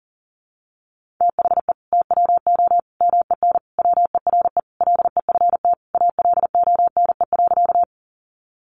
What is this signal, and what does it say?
Morse code, heard as a keyed tone, about 40 wpm, 700 Hz. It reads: THE TWO MEN WERE LEFT ALONE.
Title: Sentences from Top 1000 Words 40wpm